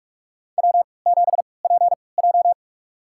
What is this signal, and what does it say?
Morse code, heard as a keyed tone, about 46 wpm, spaced as for 25 wpm, 700 Hz. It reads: W7PJ
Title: USA Call Signs - 4 Characters - 3x Character Spacing - 45wpm